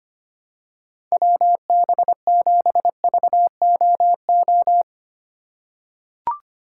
Morse code, heard as a keyed tone, about 25 wpm, 700 Hz. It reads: WB7VOO
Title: USA Call Signs - 6 Characters - 2x Character Spacing - 25wpm